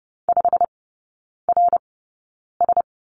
Morse code, heard as a keyed tone, about 30 wpm, spaced as for 9 wpm, 690 Hz. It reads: 5RS
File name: Lesson 16e - Letter c - Three Characters 30wpm